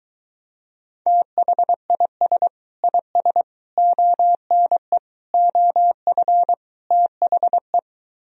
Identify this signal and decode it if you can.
Morse code, heard as a keyed tone, about 23 wpm, 700 Hz. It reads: THIS IS ONE OF THE